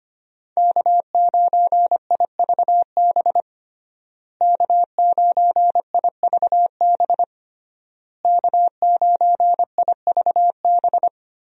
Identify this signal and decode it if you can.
Morse code, heard as a keyed tone, about 25 wpm, 700 Hz. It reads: K9IVB K9IVB K9IVB
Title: USA Call Signs - 5 Characters - Repeated 3x - 25wpm